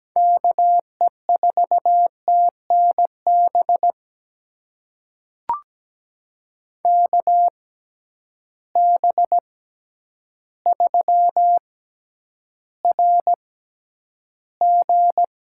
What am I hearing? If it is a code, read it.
Morse code, heard as a keyed tone, about 17 wpm, spaced as for 9 wpm, 700 Hz. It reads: KE4TNB KB3RG
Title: USA Call Signs - 6 Characters - 6x Character Spacing - 17wpm